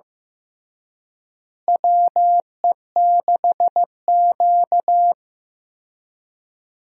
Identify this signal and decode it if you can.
Morse code, heard as a keyed tone, about 15 wpm, 700 Hz. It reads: WE6Q
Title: USA Call Signs - 4 Characters - Repeated 2x - 15wpm